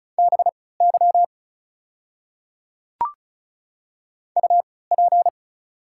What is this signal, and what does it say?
Morse code, heard as a keyed tone, about 36 wpm, spaced as for 20 wpm, 700 Hz. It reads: BY UP
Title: 3x Character Spacing - Top 200 Words - 35wpm